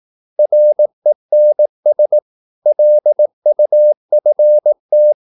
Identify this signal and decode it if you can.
Morse code, heard as a keyed tone, about 18 wpm, 600 Hz. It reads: RENS LUFT
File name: Korte ord DK 18wpm